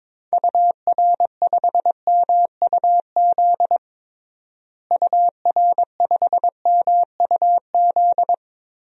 Morse code, heard as a keyed tone, about 22 wpm, 700 Hz. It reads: UR5MUZ UR5MUZ
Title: DXCC-100 Call Signs - 6 Characters - Repeated 2x - 22wpm